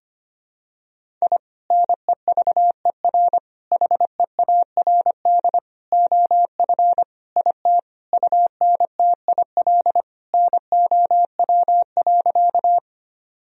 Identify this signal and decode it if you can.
Morse code, heard as a keyed tone, about 25 wpm, 700 Hz. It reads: I NEVER HEARD OF IT UNTIL NOW.
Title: Sentences from Top 1000 Words - Review Part 1 - 25wpm